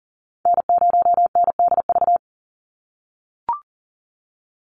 Morse code, heard as a keyed tone, about 40 wpm, 700 Hz. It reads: N0NDV
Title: USA Call Signs 40wpm